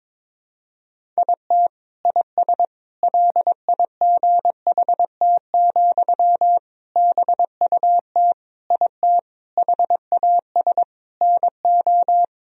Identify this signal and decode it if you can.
Morse code, heard as a keyed tone, about 22 wpm, 700 Hz. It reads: IT IS LIGHT, BUT IT HAS NO
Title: Sentences from Top 1000 Words - Review Part 1 - 22wpm